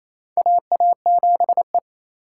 Morse code, heard as a keyed tone, about 28 wpm, 700 Hz. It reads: AA7E
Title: USA Call Signs - 4 Characters - 2x Character Spacing - 28wpm